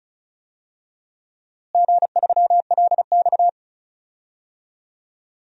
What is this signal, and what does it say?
Morse code, heard as a keyed tone, about 36 wpm, 700 Hz. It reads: G3LX
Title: DXCC-100 Call Signs - 4 Characters - Repeated 2x - 35wpm